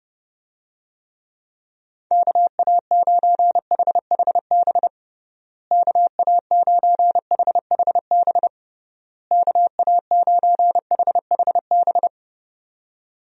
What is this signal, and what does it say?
Morse code, heard as a keyed tone, about 30 wpm, 700 Hz. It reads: KA9HHB KA9HHB KA9HHB
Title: USA Call Signs - 6 Characters - Repeated 3x - 30wpm